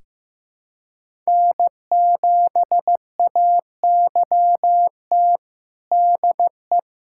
Morse code, heard as a keyed tone, about 15 wpm, 700 Hz. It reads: N7AYT DE
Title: USA Callsign DE Callsign - 5 Characters - 15wpm